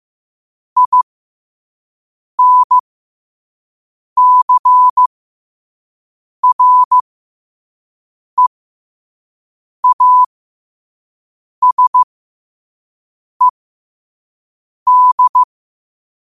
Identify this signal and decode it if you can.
Morse code, heard as a keyed tone, about 15 wpm, spaced as for 5 wpm, 1000 Hz. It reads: INCREASED
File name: qotdfiles / 5wpm